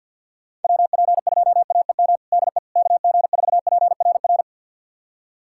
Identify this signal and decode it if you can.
Morse code, heard as a keyed tone, about 51 wpm, 700 Hz. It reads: WP2AEM DE KG4PRR
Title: USA Callsign DE Callsign - 6 Characters - 50wpm